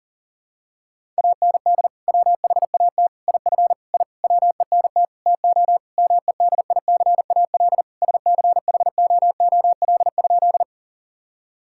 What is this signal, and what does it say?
Morse code, heard as a keyed tone, about 40 wpm, 700 Hz. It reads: AND WHAT IF I WENT TO MEDICAL SCHOOL?
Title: Sentences from Top 1000 Words - Review Part 5 - 40wpm